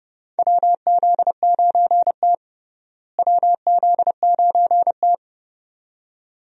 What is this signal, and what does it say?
Morse code, heard as a keyed tone, about 31 wpm, 700 Hz. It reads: WZ9T WZ9T
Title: USA Call Signs - 4 Characters - Repeated 2x - 30wpm